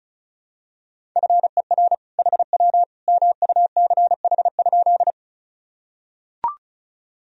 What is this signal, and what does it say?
Morse code, heard as a keyed tone, about 34 wpm, 690 Hz. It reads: FER HW MUCH?
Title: Sentences from Top 200 Words and Abbreviations 35wpm